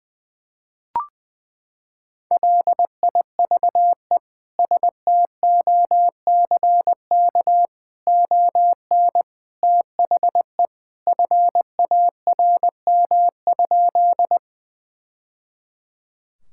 Morse code, heard as a keyed tone, about 20 wpm, 700 Hz. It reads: LIVE STOCK ON THE FARM?
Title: Sentences from Top 1000 Words - Review Part 5 - 20wpm